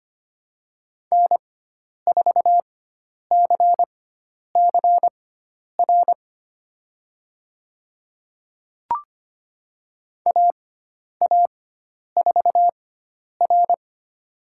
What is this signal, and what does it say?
Morse code, heard as a keyed tone, about 25 wpm, spaced as for 10 wpm, 700 Hz. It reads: N4CCR AA4R
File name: Lesson 16g - Letter c - Call Signs 25wpm